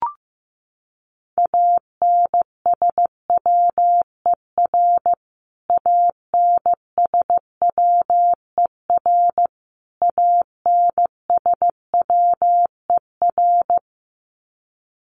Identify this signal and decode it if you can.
Morse code, heard as a keyed tone, about 15 wpm, 700 Hz. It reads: ANSWER ANSWER ANSWER
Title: Top 500 Words Repeated 15wpm